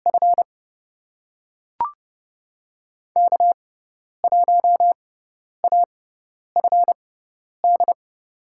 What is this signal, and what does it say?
Morse code, heard as a keyed tone, about 30 wpm, spaced as for 10 wpm, 700 Hz. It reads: F K1AFD
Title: USA Call Signs - 5 Characters - 6x Character Spacing - 30wpm